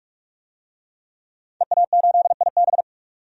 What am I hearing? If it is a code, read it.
Morse code, heard as a keyed tone, about 43 wpm, 700 Hz. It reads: EA8IB
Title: DXCC-100 Call Signs - 5 Characters - 45wpm